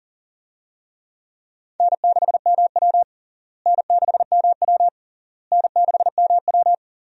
Morse code, heard as a keyed tone, about 37 wpm, 700 Hz. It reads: N6MW N6MW N6MW
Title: USA Call Signs - 4 Characters - Repeated 3x - 40wpm